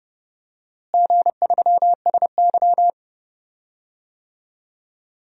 Morse code, heard as a keyed tone, about 30 wpm, 700 Hz. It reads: G3SY
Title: DXCC-100 Call Signs - 4 Characters - Repeated 3x - 30wpm